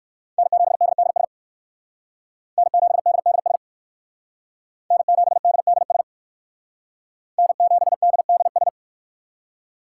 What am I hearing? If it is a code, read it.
Morse code, heard as a keyed tone, about 57 wpm, 700 Hz. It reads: N7DDS N7DDS N7DDS N7DDS